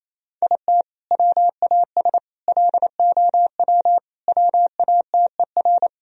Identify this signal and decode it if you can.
Morse code, heard as a keyed tone, about 28 wpm, 700 Hz. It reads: IT WAS LOW WATER
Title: Sentences from Top 1000 Words - Review Part 2 - 28wpm